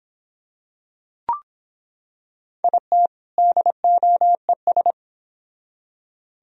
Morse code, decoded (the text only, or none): IT DOES